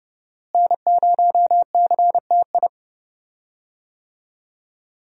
Morse code, heard as a keyed tone, about 30 wpm, 700 Hz. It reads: N0CTI
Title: USA Call Signs - 5 Characters - Repeated 3x - 30wpm